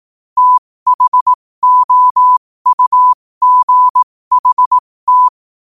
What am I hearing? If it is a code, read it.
Morse code, heard as a keyed tone, about 18 wpm, spaced as for 15 wpm, 1000 Hz. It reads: THOUGHT
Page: New quotes every day in morse code at 15 Words per minute.